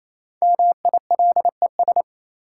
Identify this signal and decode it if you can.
Morse code, heard as a keyed tone, about 27 wpm, 700 Hz. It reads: MILES
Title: Sentences from Top 1000 Words - Review Part 3 - New Word Drill - 28wpm